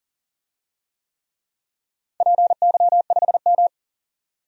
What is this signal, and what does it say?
Morse code, heard as a keyed tone, about 40 wpm, 700 Hz. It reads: PY5M
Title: DXCC-100 Call Signs - 4 Characters - 40wpm